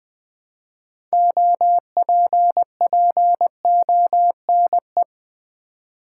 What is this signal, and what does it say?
Morse code, heard as a keyed tone, about 20 wpm, 700 Hz. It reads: OPPONE
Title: Build A Word - 3 Letters At A Time - 8-Letter Words - 20wpm